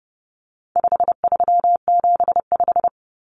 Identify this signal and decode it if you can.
Morse code, heard as a keyed tone, about 30 wpm, 700 Hz. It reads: 5375
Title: Common Cut Numbers - 4-Digit 30wpm